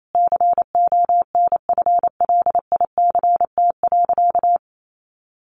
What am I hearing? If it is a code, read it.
Morse code, heard as a keyed tone, about 28 wpm, 700 Hz. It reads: CONFLICT.
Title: Sentences from Top 1300 Words 28wpm